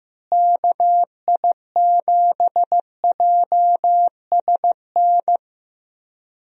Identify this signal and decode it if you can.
Morse code, heard as a keyed tone, about 15 wpm, 700 Hz. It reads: KI7JSN
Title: USA Call Signs - 6 Characters - Repeated 3x - 15wpm